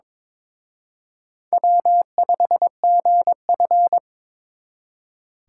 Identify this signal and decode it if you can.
Morse code, heard as a keyed tone, about 26 wpm, 700 Hz. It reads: W5GF
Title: Speed Racing - USA 1x2 Call Signs 22wpm